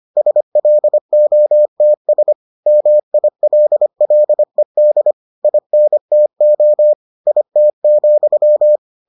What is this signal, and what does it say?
Morse code, heard as a keyed tone, about 25 wpm, 600 Hz. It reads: SLOTS MILLED INTO IT,